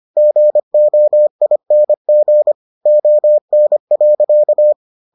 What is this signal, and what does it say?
Morse code, heard as a keyed tone, about 25 wpm, 600 Hz. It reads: GOING ON.